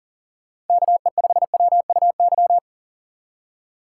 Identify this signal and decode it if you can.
Morse code, heard as a keyed tone, about 50 wpm, 700 Hz. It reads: KE5WUY
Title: Speed Racing - USA 2x3 Call Signs 40wpm